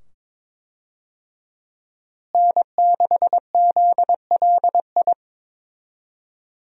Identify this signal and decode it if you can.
Morse code, heard as a keyed tone, about 22 wpm, 700 Hz. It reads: N6ZLI
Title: USA Call Signs - 5 Characters - Repeated 2x - 22wpm